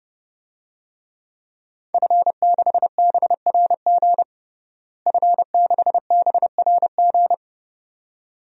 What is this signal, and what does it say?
Morse code, heard as a keyed tone, about 30 wpm, 700 Hz. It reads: F6BRG F6BRG
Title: DXCC-100 Call Signs - 5 Characters - Repeated 2x - 30wpm